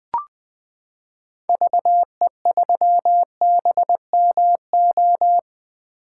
Morse code, heard as a keyed tone, about 20 wpm, 700 Hz. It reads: VE3BMO
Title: 2023 POTA Activator Call Signs - 20wpm